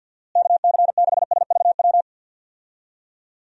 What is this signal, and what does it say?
Morse code, heard as a keyed tone, about 50 wpm, 700 Hz. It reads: KC6SVW
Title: USA Call Signs - 6 Characters - Repeated 2x - 50wpm